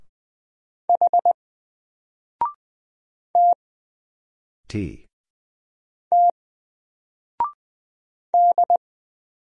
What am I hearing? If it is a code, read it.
Morse code, heard as a keyed tone, about 20 wpm, 700 Hz. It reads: H T T D
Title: Single Letter-Number Alt 20wpm